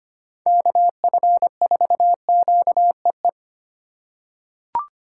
Morse code, heard as a keyed tone, about 25 wpm, 700 Hz. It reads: KF4QEE
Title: USA Call Signs - 6 Characters - 3x Character Spacing - 25wpm